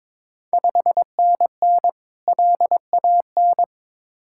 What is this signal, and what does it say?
Morse code, heard as a keyed tone, about 22 wpm, 700 Hz. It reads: 5NN LAN
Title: Ontario QSO Party Exchange 22wpm